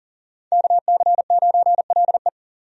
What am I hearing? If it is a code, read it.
Morse code, heard as a keyed tone, about 40 wpm, 700 Hz. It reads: KC9LE
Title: USA Call Signs - 5 Characters - 40wpm